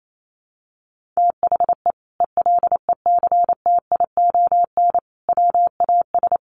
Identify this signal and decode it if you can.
Morse code, heard as a keyed tone, about 28 wpm, 700 Hz. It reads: THE ELECTION WAS
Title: Sentences from Top 1500 Words 28wpm